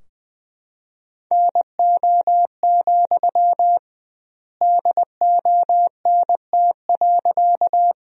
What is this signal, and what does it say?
Morse code, heard as a keyed tone, about 20 wpm, 700 Hz. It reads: NO, DONT.
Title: 2x Word-Spacing - Sentences from Top 300 Words - 20wpm